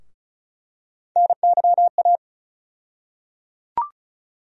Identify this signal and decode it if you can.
Morse code, heard as a keyed tone, about 34 wpm, 700 Hz. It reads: NYA
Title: USA Call Sign Postfixes - 3 Letters 35wpm